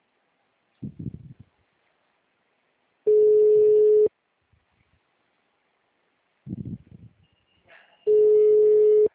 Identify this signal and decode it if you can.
Ringback tone (Europe, 425 Hz).
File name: out- -2032- - - .5197